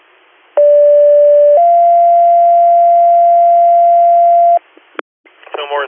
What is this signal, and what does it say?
Two-tone page, heard as a one-second tone, then a three-second tone, about 580 then 690 Hz, with speech. It is heard from 0.6 to 4.6 s.